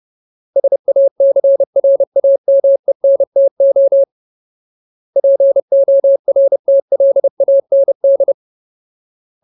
Morse code, heard as a keyed tone, about 30 wpm, 550 Hz. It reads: SACRAMENTO PORTLAND
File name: Ultimate Word List / 30WPM Ultimate